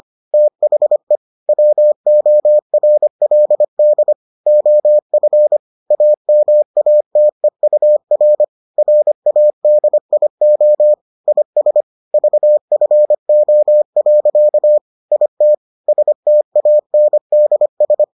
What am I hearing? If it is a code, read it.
Morse code, heard as a keyed tone, about 25 wpm, 600 Hz. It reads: THE WORLD OF AMATEUR RADIO IS VFO. IT STANDS